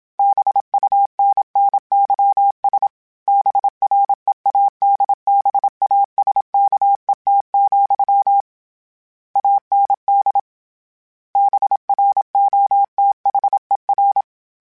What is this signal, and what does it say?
Morse code, heard as a keyed tone, about 27 wpm, 800 Hz. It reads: BUNNYS BREADBASKET, AND BROTHER